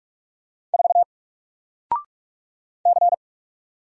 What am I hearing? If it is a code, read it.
Morse code, heard as a keyed tone, about 44 wpm, 700 Hz. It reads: 4 C
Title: Single Character (Letter:Number:Punctuation:Prosign) Alt 45wpm